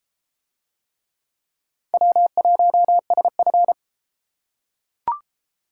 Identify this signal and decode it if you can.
Morse code, heard as a keyed tone, about 31 wpm, 700 Hz. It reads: W1SF